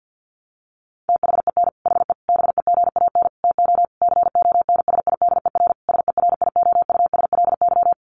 Morse code, heard as a keyed tone, about 50 wpm, 700 Hz. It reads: THEN HE BEGAN TO CONSIDER SERIOUSLY
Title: Sentences from Top 1200 Words 50wpm